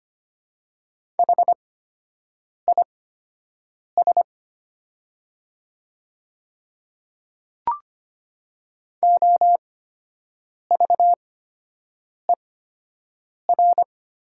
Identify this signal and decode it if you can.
Morse code, heard as a keyed tone, about 25 wpm, spaced as for 7 wpm, 700 Hz. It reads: HIS OVER
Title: 8x Character Spacing - Top 100 Words - 25wpm